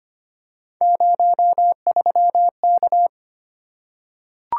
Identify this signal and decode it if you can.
Morse code, heard as a keyed tone, about 25 wpm, 700 Hz. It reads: 03K
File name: Three Letter-Number Combinations - 25wpm